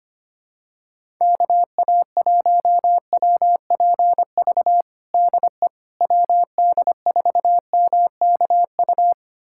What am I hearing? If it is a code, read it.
Morse code, heard as a keyed tone, about 25 wpm, 700 Hz. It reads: KA1WPV DE WD4MKU
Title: USA Callsign DE Callsign - 6 Characters - 25wpm